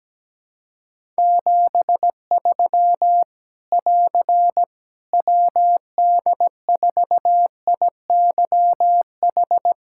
Morse code, heard as a keyed tone, about 17 wpm, 700 Hz. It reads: 73 <AR> WD4IYH
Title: Practice QSO - Exchange 4 - Ending - 17wpm